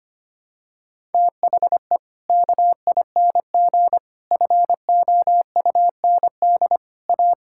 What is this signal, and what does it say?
Morse code, heard as a keyed tone, about 25 wpm, 700 Hz. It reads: THE KING FOUND A
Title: Sentences from Top 1000 Words - Review Part 5 - 25wpm